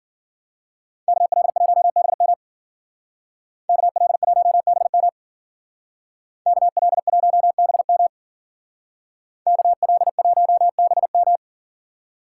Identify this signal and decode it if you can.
Morse code, heard as a keyed tone, about 48 wpm, 700 Hz. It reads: KL1BM KL1BM KL1BM KL1BM
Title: Speed Racing - USA 2x2 Call Signs 40wpm